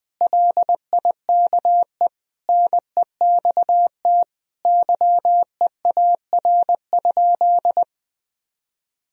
Morse code, heard as a keyed tone, about 20 wpm, 700 Hz. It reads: LIKE NEXT YEAR?
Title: Sentences from Top 1000 Words - Review Part 3 - 20wpm